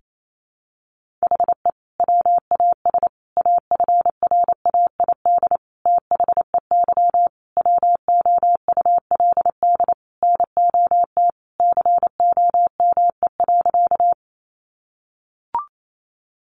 Morse code, heard as a keyed tone, about 28 wpm, 700 Hz. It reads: HE WAS AFRAID THEY WOULD NOT COME.
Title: Sentences from Top 1500 Words 28wpm